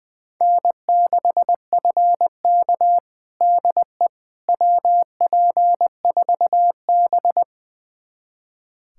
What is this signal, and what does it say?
Morse code, heard as a keyed tone, about 20 wpm, 700 Hz. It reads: N6FK DE WP4B
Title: USA Callsign DE Callsign - 4 Characters - 20wpm